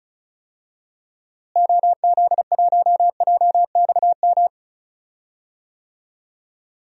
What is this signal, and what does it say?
Morse code, heard as a keyed tone, about 34 wpm, 700 Hz. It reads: OZ1JXM
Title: DXCC-100 Call Signs - 6 Characters - Repeated 2x - 35wpm